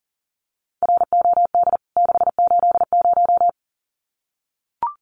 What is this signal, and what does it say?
Morse code, heard as a keyed tone, about 40 wpm, 700 Hz. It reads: ROD 680
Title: CWT Exchange 40wpm